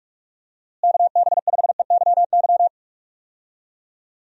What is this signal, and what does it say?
Morse code, heard as a keyed tone, about 46 wpm, 700 Hz. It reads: KB5EYY
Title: USA Call Signs - 6 Characters - Repeated 2x - 45wpm